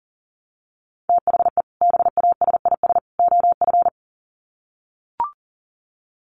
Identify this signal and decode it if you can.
Morse code, heard as a keyed tone, about 40 wpm, 700 Hz. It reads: THE BASIS OF
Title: Sets of 3 Words 40wpm